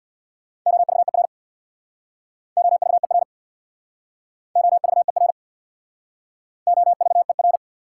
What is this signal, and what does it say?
Morse code, heard as a keyed tone, about 61 wpm, 690 Hz. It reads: OVER OVER OVER OVER
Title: Speed-Racing Prefixes and Suffixes 50wpm